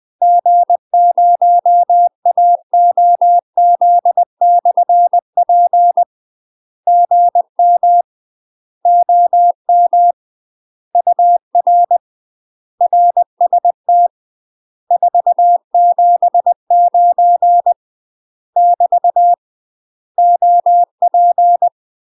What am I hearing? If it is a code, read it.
Morse code, heard as a keyed tone, about 20 wpm, 700 Hz. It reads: G0AOZ/P GM OM UR RST 479 <BT> OP
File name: UK MASTER QSOs TRAINING / 20WPM / UK MASTER QSOs 0063